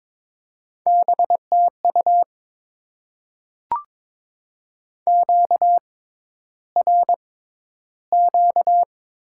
Morse code, heard as a keyed tone, about 22 wpm, spaced as for 11 wpm, 700 Hz. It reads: BTU QRQ
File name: Top 90 QSO Elements - 6x Character Spacing - 22wpm